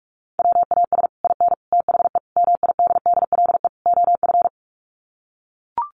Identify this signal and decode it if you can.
Morse code, heard as a keyed tone, about 44 wpm, 700 Hz. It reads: WAS IN THE MIDDLE OF
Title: Sets of 5 Words Encore 45wpm